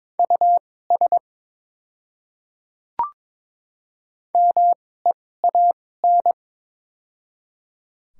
Morse code, heard as a keyed tone, about 23 wpm, spaced as for 16 wpm, 700 Hz. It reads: US MEAN
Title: 2x Character Spacing - Top 200 Words - 22wpm